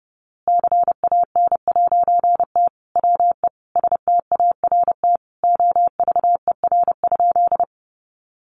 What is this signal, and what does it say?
Morse code, heard as a keyed tone, about 30 wpm, 700 Hz. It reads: CAN'T WE START OVER?
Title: A Sound Of Thunder 30wpm